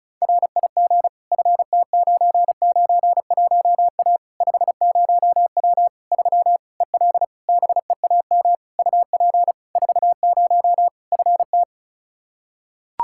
Morse code, heard as a keyed tone, about 36 wpm, 700 Hz. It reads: RIG FT991A 50W 3 EL BEAM UP 40 FT
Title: Practice QSO - Exchange 2 - Info - 35wpm